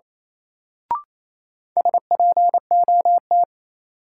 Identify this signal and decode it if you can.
Morse code, heard as a keyed tone, about 28 wpm, 700 Hz. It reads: SPOT
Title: Sentences from Top 1100 Words - New Word Drill - 28wpm